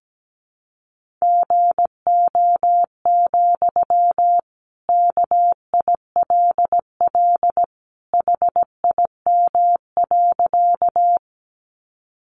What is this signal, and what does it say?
Morse code, heard as a keyed tone, about 17 wpm, 700 Hz. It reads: GO, KILL HIM.
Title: Sentences from Top 1400 Words 17wpm